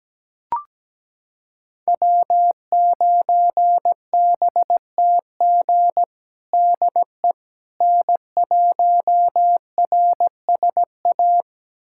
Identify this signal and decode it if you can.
Morse code, heard as a keyed tone, about 17 wpm, 700 Hz. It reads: W9BTG DE N1RSA
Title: USA Callsign DE Callsign - 5 Characters - 17wpm